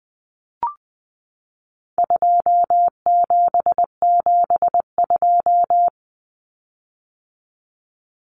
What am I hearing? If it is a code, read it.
Morse code, heard as a keyed tone, about 20 wpm, 700 Hz. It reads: 2772
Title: Common Cut Numbers - 4-Digit 20wpm